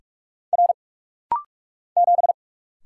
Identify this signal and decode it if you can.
Morse code, heard as a keyed tone, about 46 wpm, 700 Hz. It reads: R 7
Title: Single Letter-Number - Rapid-Fire 45wpm